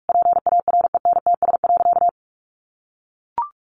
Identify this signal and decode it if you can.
Morse code, heard as a keyed tone, about 44 wpm, 700 Hz. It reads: PARENTS.
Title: Easy Phrases 45wpm